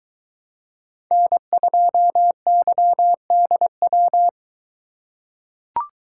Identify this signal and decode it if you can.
Morse code, heard as a keyed tone, about 22 wpm, 700 Hz. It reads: N2YDW